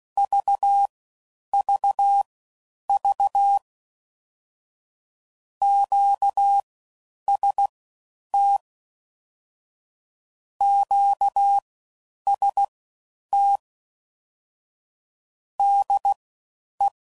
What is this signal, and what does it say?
Morse code, heard as a keyed tone, about 16 wpm, spaced as for 9 wpm, 800 Hz. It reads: VVV QST QST DE
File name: Basic Sets / QST de WA7RSO / All Dits